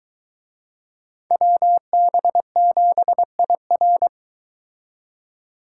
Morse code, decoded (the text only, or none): WB7IR